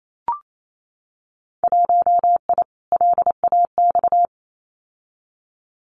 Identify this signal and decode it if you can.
Morse code, heard as a keyed tone, about 28 wpm, 700 Hz. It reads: 1I LAX
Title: Winter Field Day Exchanges Encore 28wpm